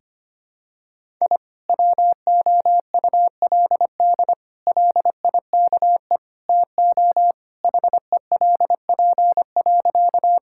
Morse code, heard as a keyed tone, about 25 wpm, 700 Hz. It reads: I WOULD LIKE TO HELP.
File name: practice / 25wpm / Sentences with 4-Word Phrases - 25wpm